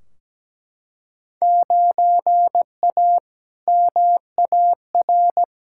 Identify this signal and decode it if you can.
Morse code, heard as a keyed tone, about 17 wpm, 700 Hz. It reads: 9A MAR
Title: Field Day Exchanges Encore - 17wpm